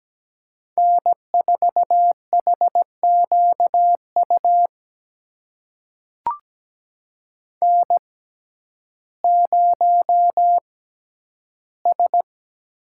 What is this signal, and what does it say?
Morse code, heard as a keyed tone, about 17 wpm, spaced as for 10 wpm, 700 Hz. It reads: N4HQU N0S
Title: USA Call Signs - 5 Characters - 6x Character Spacing - 17wpm